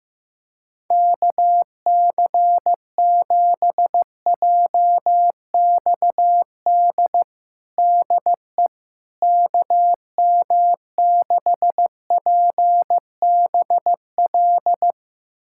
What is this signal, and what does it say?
Morse code, heard as a keyed tone, about 16 wpm, 700 Hz. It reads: KC7JXD DE KM6PBL
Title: USA Callsign DE Callsign - 6 Characters - 15wpm